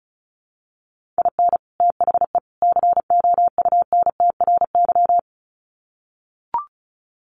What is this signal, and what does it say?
Morse code, heard as a keyed tone, about 34 wpm, 700 Hz. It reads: IN THE COUNTRY
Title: Sets of 3 Words 35wpm